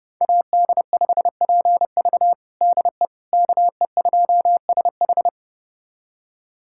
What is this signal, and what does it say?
Morse code, heard as a keyed tone, about 29 wpm, 700 Hz. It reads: AD5PV DE KE2SH
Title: USA Callsign DE Callsign - 5 Characters - 30wpm